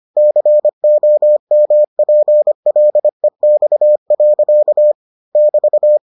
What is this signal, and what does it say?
Morse code, heard as a keyed tone, about 25 wpm, 600 Hz. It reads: COMPLEX. <BT>